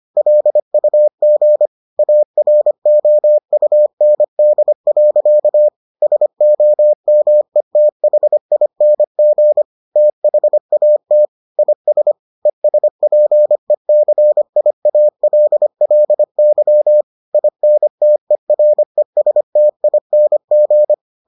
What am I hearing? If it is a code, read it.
Morse code, heard as a keyed tone, about 25 wpm, 600 Hz. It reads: LUG AROUND. SOMETHING THAT IS ESPECIALLY INTERESTING